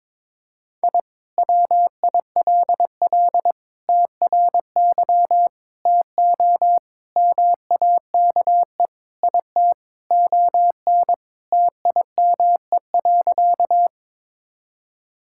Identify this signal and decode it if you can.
Morse code, heard as a keyed tone, about 22 wpm, 700 Hz. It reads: I WILL TRY TO MAKE IT ON TIME.
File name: practice / 22wpm / Sentences with 3-Word Phrases - 22wpm